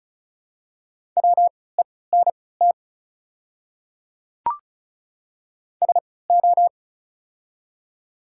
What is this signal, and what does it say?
Morse code, heard as a keyed tone, about 34 wpm, spaced as for 20 wpm, 700 Hz. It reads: WENT SO
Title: 3x Character Spacing - Top 300 Words - 35wpm